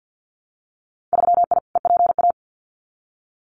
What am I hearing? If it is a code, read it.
Morse code, heard as a keyed tone, about 50 wpm, 700 Hz. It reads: 3I EPA
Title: Winter Field Day Exchanges 50wpm